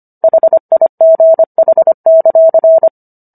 Morse code, heard as a keyed tone, about 25 wpm, 650 Hz. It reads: HIGH;